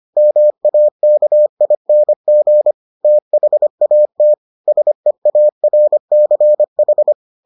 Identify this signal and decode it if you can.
Morse code, heard as a keyed tone, about 25 wpm, 600 Hz. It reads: MAKING THAT SEARCH